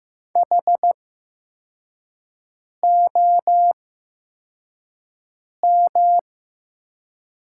Morse code, heard as a keyed tone, about 15 wpm, spaced as for 4 wpm, 700 Hz. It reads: HOM
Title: 8x Character Spacing - Top 300 Words - 15wpm